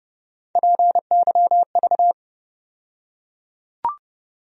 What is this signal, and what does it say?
Morse code, heard as a keyed tone, about 30 wpm, 700 Hz. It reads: PYV
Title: USA Call Sign Postfixes - 3 Letters 30wpm